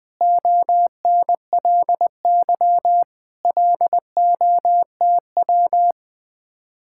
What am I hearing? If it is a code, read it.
Morse code, heard as a keyed tone, about 20 wpm, 700 Hz. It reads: ONLY LOTW
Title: Speed-Racing Common QSO Phrases - 17wpm